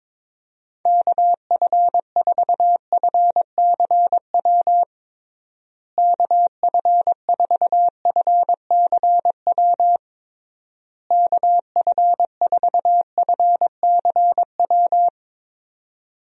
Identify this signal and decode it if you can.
Morse code, heard as a keyed tone, about 22 wpm, 700 Hz. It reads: KF4FCW KF4FCW KF4FCW
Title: USA Call Signs - 6 Characters - Repeated 3x - 22wpm